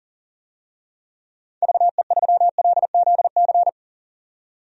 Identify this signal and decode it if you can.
Morse code, heard as a keyed tone, about 40 wpm, 700 Hz. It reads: VE3LZC
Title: DXCC-100 Call Signs - 6 Characters - 40wpm